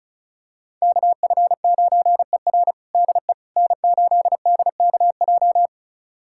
Morse code, heard as a keyed tone, about 36 wpm, 700 Hz. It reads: KF9ER DE N8DKJ
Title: USA Callsign DE Callsign 35wpm